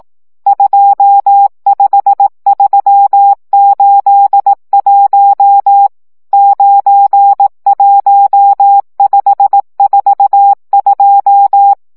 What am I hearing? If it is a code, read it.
Morse code, heard as a keyed tone, about 18 wpm, 800 Hz. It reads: 25381 91542
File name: morze vizsga 2011-1 15